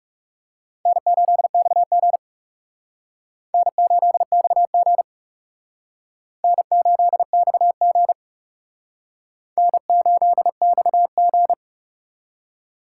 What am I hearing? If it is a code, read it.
Morse code, heard as a keyed tone, about 37 wpm, 700 Hz. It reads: N8XG N8XG N8XG N8XG
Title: Speed Racing - USA 1x2 Call Signs 30wpm